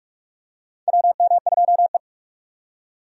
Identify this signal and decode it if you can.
Morse code, heard as a keyed tone, about 46 wpm, 700 Hz. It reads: WM2E
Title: USA Call Signs - 4 Characters - 4x Character Spacing - 45wpm